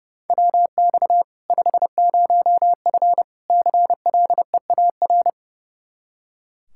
Morse code, heard as a keyed tone, about 32 wpm, 700 Hz. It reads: WX 50F CLEAR
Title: K1USN SST QSO Practice 30wpm